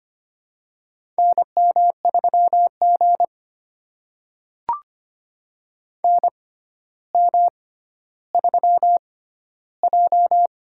Morse code, heard as a keyed tone, about 25 wpm, spaced as for 13 wpm, 700 Hz. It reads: NM3G NM3J
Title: USA Call Signs - 4 Characters - 6x Character Spacing - 25wpm